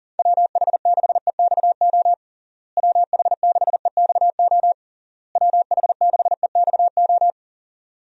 Morse code, heard as a keyed tone, about 40 wpm, 700 Hz. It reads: WH6EXO WH6EXO WH6EXO
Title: USA Call Signs - 6 Characters - Repeated 3x - 40wpm